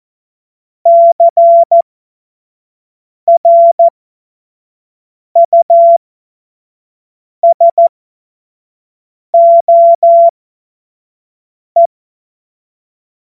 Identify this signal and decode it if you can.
Morse code, heard as a keyed tone, about 14 wpm, spaced as for 5 wpm, 680 Hz. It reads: CRUSOE
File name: Morsekurs / Witze / Witz 20 Mai 2025 25 F70